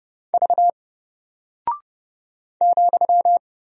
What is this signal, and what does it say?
Morse code, heard as a keyed tone, about 29 wpm, 700 Hz. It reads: V ,
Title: Single Character (Letter:Number:Punctuation:Prosign) 30wpm